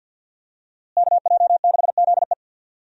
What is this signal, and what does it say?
Morse code, heard as a keyed tone, about 47 wpm, 700 Hz. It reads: KJ6ZE
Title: USA Call Signs - 5 Characters - Repeated 3x - 50wpm